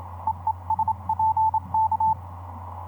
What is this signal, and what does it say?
Morse code, heard as a keyed tone, about 27 wpm, 880 Hz, with radio noise. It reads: EESPK
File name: 15m cw